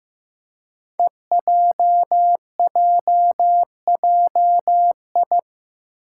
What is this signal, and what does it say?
Morse code, heard as a keyed tone, about 15 wpm, 700 Hz. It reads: EJJJI
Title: Five Challenging Letter-Number Combinations 15wpm